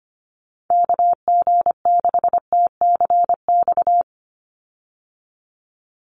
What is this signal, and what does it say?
Morse code, heard as a keyed tone, about 25 wpm, 700 Hz. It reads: KG6TCX
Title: USA Call Signs 25wpm